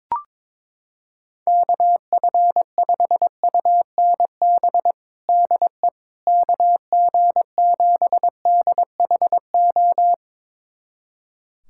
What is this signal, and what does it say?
Morse code, heard as a keyed tone, about 22 wpm, 700 Hz. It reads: KF5UNB DE KG7DHO
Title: USA Callsign DE Callsign - 6 Characters - 22wpm